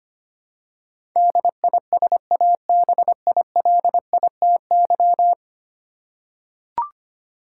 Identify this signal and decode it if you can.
Morse code, heard as a keyed tone, about 25 wpm, 700 Hz. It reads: DISABILITY
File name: 10 Letter Words - First 3 Letters Given - 25wpm